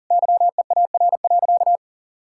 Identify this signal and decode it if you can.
Morse code, heard as a keyed tone, about 37 wpm, 700 Hz. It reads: YEAR.
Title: Sentences from Top 1000 Words - Review Part 5 - 40wpm